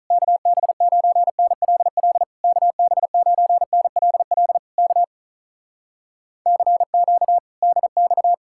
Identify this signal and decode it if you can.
Morse code, heard as a keyed tone, about 48 wpm, 700 Hz. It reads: KB9NLL KB9NLL K CQ DX